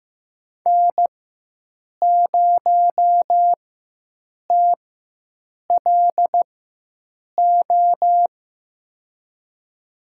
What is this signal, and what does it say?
Morse code, heard as a keyed tone, about 15 wpm, spaced as for 7 wpm, 700 Hz. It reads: N0TLO
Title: USA Call Signs - 5 Characters - 4x Character Spacing - 15wpm